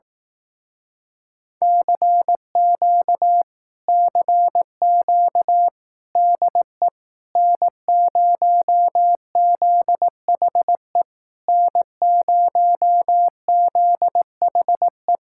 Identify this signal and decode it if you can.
Morse code, heard as a keyed tone, about 18 wpm, 700 Hz. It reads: CQ CQ DE N0ZHE N0ZHE
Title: Speed-Racing Common QSO Phrases Encore - 15wpm